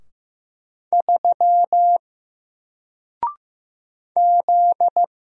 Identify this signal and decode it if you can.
Morse code, heard as a keyed tone, about 15 wpm, 700 Hz. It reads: 3 Z
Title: Single Letter-Number 15wpm